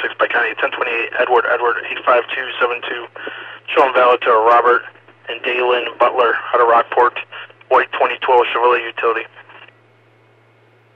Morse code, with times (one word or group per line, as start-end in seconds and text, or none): none